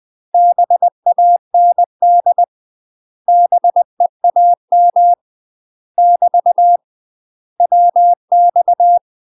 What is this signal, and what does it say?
Morse code, heard as a keyed tone, about 20 wpm, 700 Hz. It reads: BAND BEAM <BT> WX
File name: UK MASTER QSOs TRAINING/20WPM